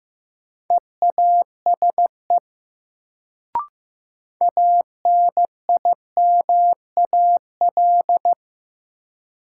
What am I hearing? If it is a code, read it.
Morse code, heard as a keyed tone, about 15 wpm, 700 Hz. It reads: EASE ANIMAL
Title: Sentences from Top 1000 Words - Review Part 1 - New Word Drill - 15wpm